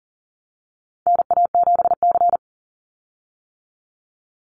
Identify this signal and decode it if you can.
Morse code, heard as a keyed tone, about 38 wpm, 700 Hz. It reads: NA7C
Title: Call Signs - 40wpm